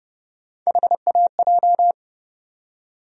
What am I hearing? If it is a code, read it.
Morse code, heard as a keyed tone, about 30 wpm, 700 Hz. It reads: HAJ
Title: USA Call Sign Postfixes - 3 Letters 30wpm